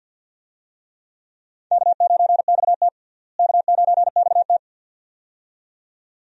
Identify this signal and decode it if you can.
Morse code, heard as a keyed tone, about 45 wpm, 700 Hz. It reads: K9XT K9XT
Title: USA Call Signs - 4 Characters - Repeated 2x - 50wpm